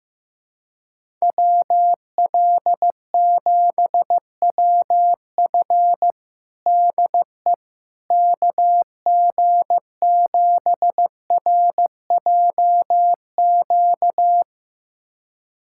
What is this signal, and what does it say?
Morse code, heard as a keyed tone, about 15 wpm, 700 Hz. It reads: WL7WF DE KG7RJQ
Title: USA Callsign DE Callsign 15wpm